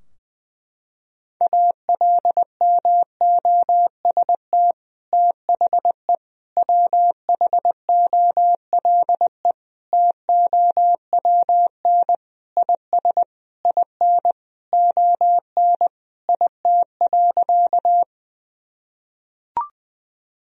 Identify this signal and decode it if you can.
Morse code, heard as a keyed tone, about 19 wpm, 700 Hz. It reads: ALMOST THE WHOLE TOWN IS IN ON IT.
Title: Sentences from Top 1000 Words - Review Part 4 - 20wpm